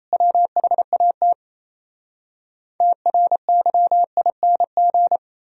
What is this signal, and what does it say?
Morse code, heard as a keyed tone, about 37 wpm, 700 Hz. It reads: WHAT TRYING